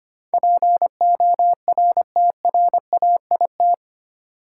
Morse code, heard as a keyed tone, about 25 wpm, 700 Hz. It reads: PORTRAIT
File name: Eight Letter Words - 25wpm